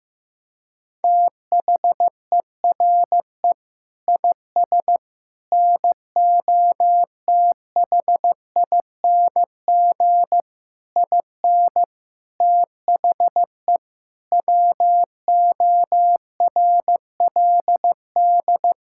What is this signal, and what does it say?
Morse code, heard as a keyed tone, about 16 wpm, 700 Hz. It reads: THERE IS NOTHING IN THE WORLD
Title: Sentences from Top 1600 Words 15wpm